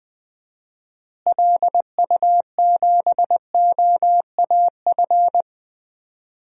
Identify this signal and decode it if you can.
Morse code, heard as a keyed tone, about 20 wpm, 700 Hz. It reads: LU7OAF
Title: DXCC-100 Call Signs - 6 Characters - Repeated 2x - 20wpm